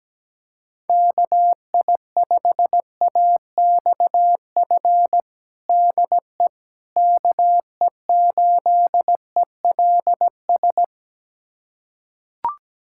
Morse code, heard as a keyed tone, about 17 wpm, 700 Hz. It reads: KI5AXF DE KE8ELS
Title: USA Callsign DE Callsign - 6 Characters - 17wpm